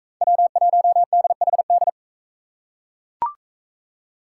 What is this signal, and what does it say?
Morse code, heard as a keyed tone, about 39 wpm, 700 Hz. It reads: W1DHD